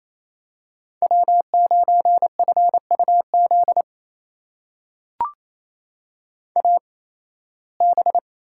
Morse code, heard as a keyed tone, about 28 wpm, spaced as for 18 wpm, 700 Hz. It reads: W9FUZ AB
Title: USA Call Signs - 5 Characters - 8x Character Spacing - 28wpm